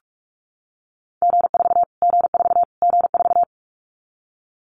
Morse code, heard as a keyed tone, about 44 wpm, 700 Hz. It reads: G4 G4 G4
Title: Callsign Prefixes for Top 25 DXCC - Repeated 3x - 45wpm